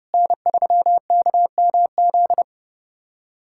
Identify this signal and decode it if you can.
Morse code, heard as a keyed tone, about 30 wpm, 700 Hz. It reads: N3KMZ
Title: USA Callsign DE Callsign - 5 Characters - 30wpm